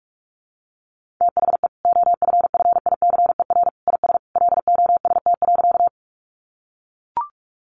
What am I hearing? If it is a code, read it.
Morse code, heard as a keyed tone, about 44 wpm, 700 Hz. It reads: THE OFFICER IS LOST.
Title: Sentences from Top 1400 Words 45wpm